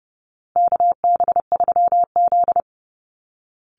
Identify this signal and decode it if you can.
Morse code, heard as a keyed tone, about 30 wpm, 700 Hz. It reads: KB3Z
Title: Call Signs Repeated 2x - 30wpm